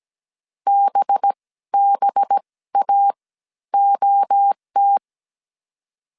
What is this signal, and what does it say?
Morse code, heard as a keyed tone, about 17 wpm, spaced as for 14 wpm, 800 Hz. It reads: BBA OT
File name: 16wpm